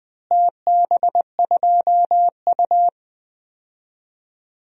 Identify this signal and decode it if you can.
Morse code, heard as a keyed tone, about 20 wpm, 700 Hz. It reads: TB2U
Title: DXCC-100 Call Signs - 4 Characters - Repeated 3x - 20wpm